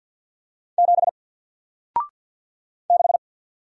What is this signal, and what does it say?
Morse code, heard as a keyed tone, about 49 wpm, 700 Hz. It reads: 7 6
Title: Single Letter-Number 50wpm